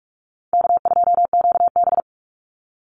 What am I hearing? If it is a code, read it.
Morse code, heard as a keyed tone, about 44 wpm, 700 Hz. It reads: K2QB
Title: Call Signs Repeated 2x - 45wpm